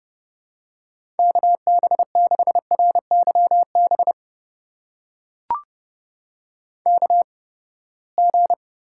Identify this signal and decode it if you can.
Morse code, heard as a keyed tone, about 30 wpm, spaced as for 21 wpm, 700 Hz. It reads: KB6RYB KG
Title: USA Call Signs - 6 Characters - 8x Character Spacing - 30wpm